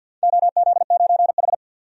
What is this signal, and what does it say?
Morse code, heard as a keyed tone, about 51 wpm, 700 Hz. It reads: OZ9H
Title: DXCC-100 Call Signs - 4 Characters - 50wpm